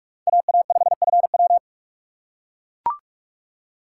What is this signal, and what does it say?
Morse code, heard as a keyed tone, about 44 wpm, 700 Hz. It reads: AA5FW